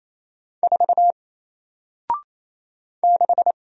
Morse code, heard as a keyed tone, about 27 wpm, 700 Hz. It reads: 4 6
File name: Single-Letter-Number-28wpm